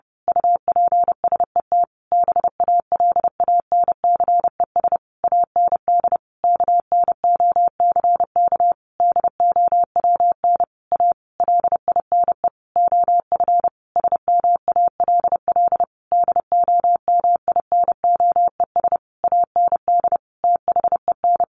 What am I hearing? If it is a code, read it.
Morse code, heard as a keyed tone, about 30 wpm, 700 Hz. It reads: UPSET BALANCES AND KNOCK DOWN A LINE OF SMALL DOMINOES AND THEN
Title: A Sound Of Thunder 30wpm